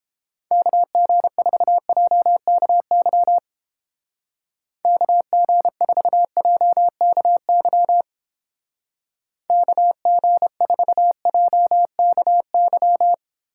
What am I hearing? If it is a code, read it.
Morse code, heard as a keyed tone, about 31 wpm, 700 Hz. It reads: KG4JKY KG4JKY KG4JKY